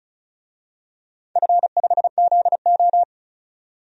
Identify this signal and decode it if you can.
Morse code, heard as a keyed tone, about 34 wpm, 700 Hz. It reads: F5ZO
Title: DXCC-100 Call Signs - 4 Characters - 35wpm